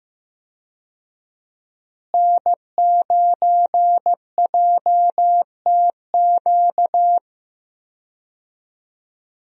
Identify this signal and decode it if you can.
Morse code, heard as a keyed tone, about 15 wpm, 700 Hz. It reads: N9JTQ
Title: USA Call Signs - 5 Characters - Repeated 3x - 15wpm